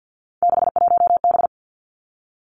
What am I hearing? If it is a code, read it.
Morse code, heard as a keyed tone, about 50 wpm, 700 Hz. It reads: 61B
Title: Three Challenging Letter-Number Combinations 50wpm